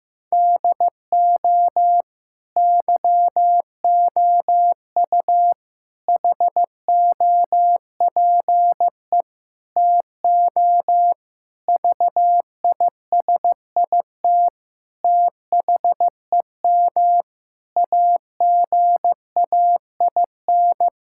Morse code, heard as a keyed tone, about 16 wpm, 700 Hz. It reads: DO YOU HOPE TO VISIT THEM AGAIN
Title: Sentences from Top 1000 Words - Review Part 6 - 15wpm